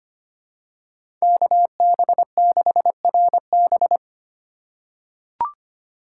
Morse code, heard as a keyed tone, about 25 wpm, 700 Hz. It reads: KB6RB
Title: USA Call Signs - 6x Character Spacing - 25wpm